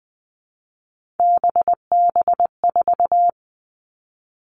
Morse code, heard as a keyed tone, about 20 wpm, 700 Hz. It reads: BB4
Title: All Cut Numbers - 3-Digit 20wpm